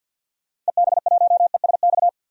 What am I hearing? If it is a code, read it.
Morse code, heard as a keyed tone, about 52 wpm, 700 Hz. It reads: EB1ESX
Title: DXCC-100 Call Signs - 6 Characters - 50wpm